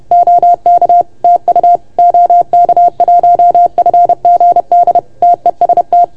Morse code, heard as a keyed tone, about 30 wpm, 690 Hz. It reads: OK TU OK1FGD TEST
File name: Archiv / CUC381 / OK1FGD CUC 381